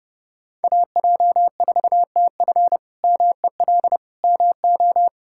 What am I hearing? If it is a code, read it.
Morse code, heard as a keyed tone, about 30 wpm, 700 Hz. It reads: AJ4TF MEL MO
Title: K1USN SST QSO Practice - 30wpm